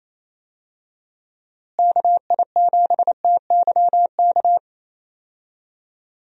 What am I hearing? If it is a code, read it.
Morse code, heard as a keyed tone, about 28 wpm, 700 Hz. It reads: KI7TYK
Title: USA Call Signs - 6 Characters - Repeated 2x - 28wpm